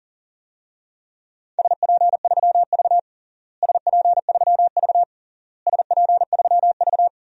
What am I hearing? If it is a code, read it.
Morse code, heard as a keyed tone, about 37 wpm, 700 Hz. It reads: SP3V SP3V SP3V
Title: DXCC-100 Call Signs - 4 Characters - Repeated 3x - 40wpm